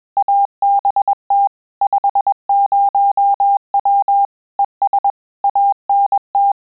Morse code, heard as a keyed tone, about 21 wpm, 800 Hz. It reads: ABT 50W ES ANT
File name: QSO CW 100 CPM